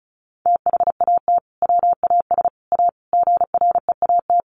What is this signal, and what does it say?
Morse code, heard as a keyed tone, about 34 wpm, 700 Hz. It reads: THAT WAS A GREAT
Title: Sentences from Top 1000 Words 35wpm